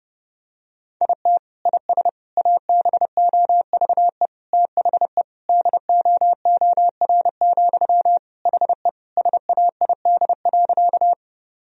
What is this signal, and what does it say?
Morse code, heard as a keyed tone, about 32 wpm, 700 Hz. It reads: IT IS ABOVE THE DOOR, HE SAID.
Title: Sentences from Top 1000 Words - Review Part 2 - 30wpm